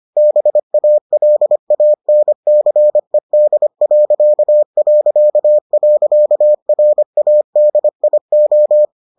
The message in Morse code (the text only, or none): BALANCED...RADIO